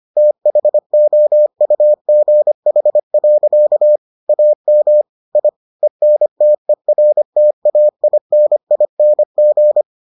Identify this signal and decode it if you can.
Morse code, heard as a keyed tone, about 25 wpm, 600 Hz. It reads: THOUGH. AM I ENTERTAINING